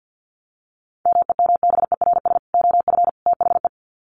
Morse code, heard as a keyed tone, about 48 wpm, 700 Hz. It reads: MEMBERS OF THE
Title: Sets of 3 Words 50wpm